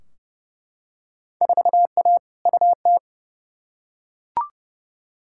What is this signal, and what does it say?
Morse code, heard as a keyed tone, about 30 wpm, 700 Hz. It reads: 4A UT
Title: Field Day Exchanges - 30wpm